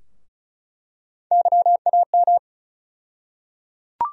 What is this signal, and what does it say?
Morse code, heard as a keyed tone, about 36 wpm, 700 Hz. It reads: YAM
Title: USA Call Sign Postfixes - 3 Letters 35wpm